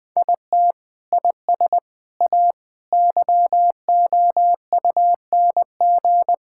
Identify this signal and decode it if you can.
Morse code, heard as a keyed tone, about 21 wpm, 700 Hz. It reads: IT IS A YOUNG
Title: Sentences from Top 1000 Words - Review Part 6 - 20wpm